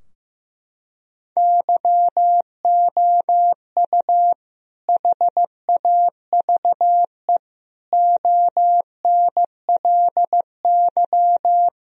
Morse code, heard as a keyed tone, about 16 wpm, 700 Hz. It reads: YOU HAVE ONLY
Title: Sentences from Top 1000 Words - Review Part 2 - 15wpm